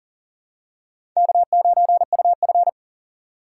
Morse code, heard as a keyed tone, about 40 wpm, 700 Hz. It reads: K9UF
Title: USA Call Signs - 2x Character Spacing - 40wpm